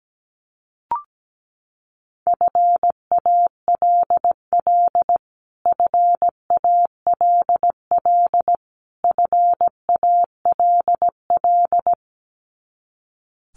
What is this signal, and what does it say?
Morse code, heard as a keyed tone, about 17 wpm, 700 Hz. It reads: FALL FALL FALL
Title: Top 500 Words Repeated 17wpm